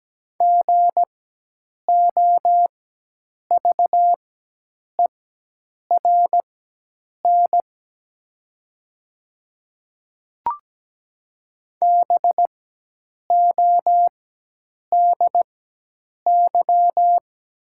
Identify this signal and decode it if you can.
Morse code, heard as a keyed tone, about 17 wpm, spaced as for 8 wpm, 700 Hz. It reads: GOVERN BODY
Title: 4x Character Spacing - Top 500 Words - 17wpm